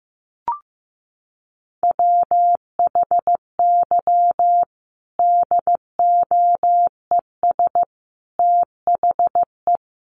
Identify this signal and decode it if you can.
Morse code, heard as a keyed tone, about 14 wpm, 700 Hz. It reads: WHY DOES THE
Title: Sentences from Top 500 Words 15wpm